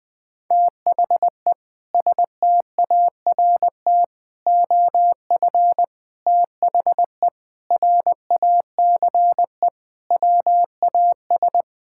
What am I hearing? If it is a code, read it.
Morse code, heard as a keyed tone, about 20 wpm, 700 Hz. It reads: THE START OF THE RACE WAS
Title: Sentences with 4-Word Phrases - Spoken Beginning - 20wpm